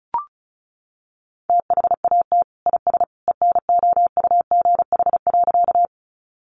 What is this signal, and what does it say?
Morse code, heard as a keyed tone, about 34 wpm, 700 Hz. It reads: THAT IS ENOUGH.
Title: Sentences from Top 400 Words 35wpm